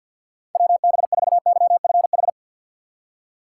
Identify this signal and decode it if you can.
Morse code, heard as a keyed tone, about 53 wpm, 700 Hz. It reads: WB4YFH
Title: USA Call Signs - 6 Characters - 4x Character Spacing - 50wpm